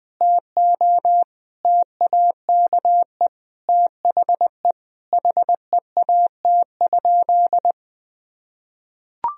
Morse code, heard as a keyed tone, about 20 wpm, 700 Hz. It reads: TO TAKE THE HEAT?
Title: Sentences from Top 1000 Words - Review Part 3 - 20wpm